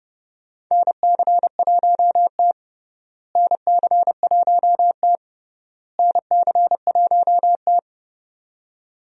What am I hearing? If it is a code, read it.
Morse code, heard as a keyed tone, about 30 wpm, 700 Hz. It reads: NC1T NC1T NC1T
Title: USA Call Signs - 4 Characters - Repeated 3x - 30wpm